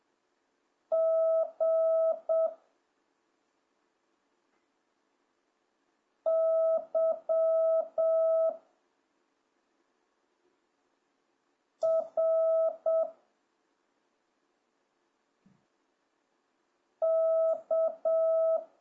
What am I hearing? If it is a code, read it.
Morse code, heard as a keyed tone, about 7 wpm, 650 Hz. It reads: G Y R K